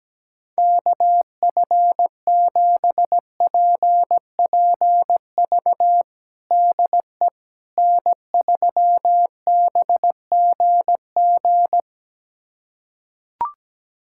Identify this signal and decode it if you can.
Morse code, heard as a keyed tone, about 17 wpm, 700 Hz. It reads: KF7PPV DE N3BGG
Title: USA Callsign DE Callsign 17wpm